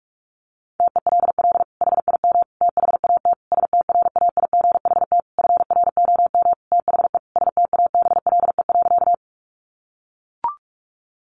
Morse code, heard as a keyed tone, about 44 wpm, 700 Hz. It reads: TELL HIM THAT STRAIGHT FROM THE STABLE.
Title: Sentences from Top 1500 Words 45wpm